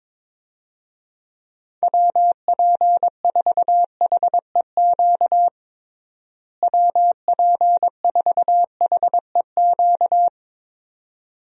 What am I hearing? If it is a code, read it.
Morse code, heard as a keyed tone, about 22 wpm, 700 Hz. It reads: WP4HEQ WP4HEQ
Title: USA Call Signs - 6 Characters - Repeated 2x - 22wpm